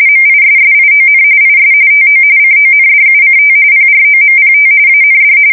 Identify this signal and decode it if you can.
RTTY, about 45.45 baud. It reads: MISSION IS TO CONDUCT PROMPT AND
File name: fsk170 / battleshipnj